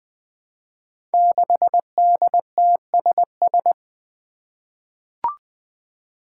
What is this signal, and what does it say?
Morse code, heard as a keyed tone, about 20 wpm, 700 Hz. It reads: 6DTSS
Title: Five Challenging Letter-Number Combinations 20wpm